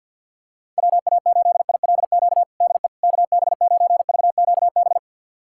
Morse code, heard as a keyed tone, about 52 wpm, 700 Hz. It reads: WA8ILQ DE KB9VQB
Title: USA Callsign DE Callsign 50wpm